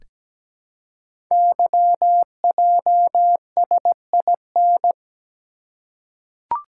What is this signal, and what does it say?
Morse code, heard as a keyed tone, about 17 wpm, 700 Hz. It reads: YJSIN
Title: Code Groups - 5 Letters - 17wpm